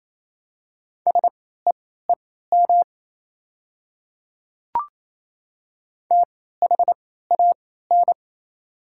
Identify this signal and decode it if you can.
Morse code, heard as a keyed tone, about 28 wpm, spaced as for 16 wpm, 700 Hz. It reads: SEEM THAN
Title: 3x Character Spacing - Top 500 Words - 28wpm